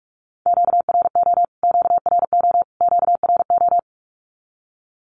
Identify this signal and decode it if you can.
Morse code, heard as a keyed tone, about 44 wpm, 700 Hz. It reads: QRO QRO QRO
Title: Q-Codes Repeated 3x 45wpm